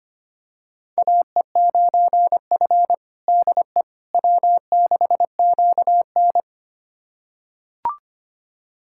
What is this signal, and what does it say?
Morse code, heard as a keyed tone, about 25 wpm, 700 Hz. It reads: AE9F DE W6QN
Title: USA Callsign DE Callsign - 4 Characters - 25wpm